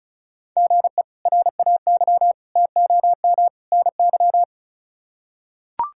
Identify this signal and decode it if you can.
Morse code, heard as a keyed tone, about 36 wpm, 700 Hz. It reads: GE RAY TOM NY
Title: K1USN SST QSO Practice - 35wpm